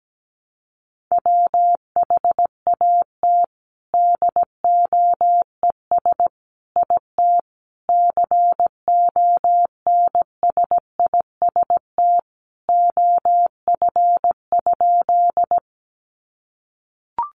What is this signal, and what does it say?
Morse code, heard as a keyed tone, about 17 wpm, 700 Hz. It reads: WHAT DOES IT CONSIST OF?
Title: Sentences from Top 1400 Words 17wpm